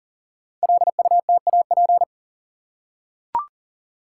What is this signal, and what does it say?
Morse code, heard as a keyed tone, about 38 wpm, 700 Hz. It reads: LUTAP
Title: Code Groups - 5 Letters - 40wpm